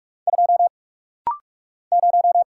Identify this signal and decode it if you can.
Morse code, heard as a keyed tone, about 46 wpm, 700 Hz. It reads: 2 0
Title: Single Letter-Number - Rapid-Fire 45wpm